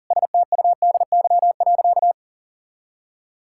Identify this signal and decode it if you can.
Morse code, heard as a keyed tone, about 40 wpm, 700 Hz. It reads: STUDY.
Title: Sentences from Top 1000 Words - Review Part 5 - 40wpm